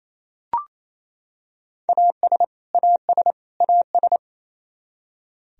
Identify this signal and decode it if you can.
Morse code, heard as a keyed tone, about 28 wpm, 700 Hz. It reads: AS AS AS
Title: Top 100 Words Repeated 28wpm